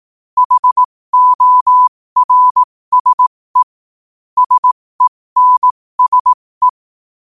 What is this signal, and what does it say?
Morse code, heard as a keyed tone, about 18 wpm, spaced as for 15 wpm, 1000 Hz. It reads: HORSE SENSE
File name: qotdfiles / 15wpm